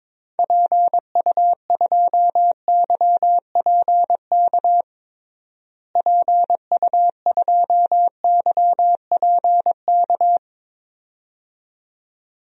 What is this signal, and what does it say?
Morse code, heard as a keyed tone, about 22 wpm, 700 Hz. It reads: PU2YPK PU2YPK
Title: DXCC-100 Call Signs - 6 Characters - Repeated 2x - 22wpm